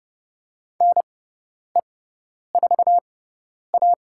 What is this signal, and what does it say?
Morse code, heard as a keyed tone, about 30 wpm, spaced as for 10 wpm, 700 Hz. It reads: NE4A
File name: Lesson 16g - Letter c - Call Signs 30wpm